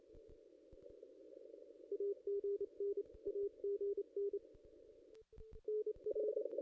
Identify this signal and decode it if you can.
Morse code, heard as a keyed tone, about 30 wpm, 440 Hz. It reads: AGN AGN NR
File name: contests / ss-cw-k5tr-op-n5rz -0400